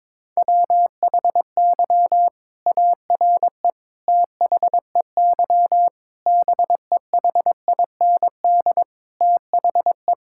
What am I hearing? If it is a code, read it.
Morse code, heard as a keyed tone, about 22 wpm, 700 Hz. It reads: WHY ARE THEY BEHIND THE
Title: Sentences from Top 1000 Words - Review Part 6 - 22wpm